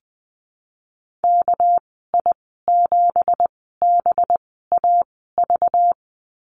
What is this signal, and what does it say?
Morse code, heard as a keyed tone, about 20 wpm, spaced as for 14 wpm, 700 Hz. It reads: KI7BAV
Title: 219 Lesson 35g - Number 0 - Call Signs 20wpm